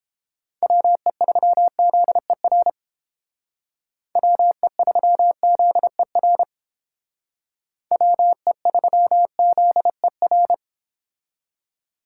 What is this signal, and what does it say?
Morse code, heard as a keyed tone, about 28 wpm, 700 Hz. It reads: WE3ZER WE3ZER WE3ZER